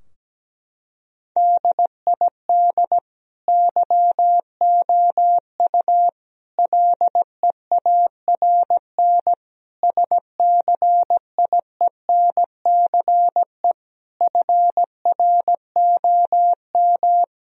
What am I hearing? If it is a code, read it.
Morse code, heard as a keyed tone, about 17 wpm, 700 Hz. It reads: DID YOU LEARN SCIENCE FROM
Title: Sentences from Top 1000 Words - Review Part 2 - 17wpm